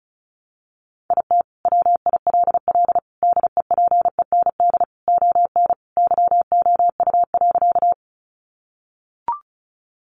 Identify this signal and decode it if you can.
Morse code, heard as a keyed tone, about 34 wpm, 700 Hz. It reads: IT WILL DEPEND ON YOU.
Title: Sentences from Top 1300 Words 35wpm